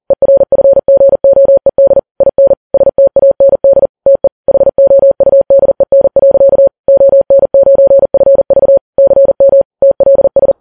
Morse code, heard as a keyed tone, about 40 wpm, 550 Hz. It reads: ERFGOED IN STAND TE HOUDEN. ON9FV CM TLS
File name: Files Web tls / Morse on9fv / morse40kps